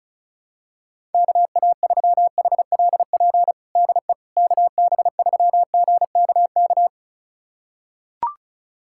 Morse code, heard as a keyed tone, about 34 wpm, 700 Hz. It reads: KA3HLP DE KB3GKK
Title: USA Callsign DE Callsign 35wpm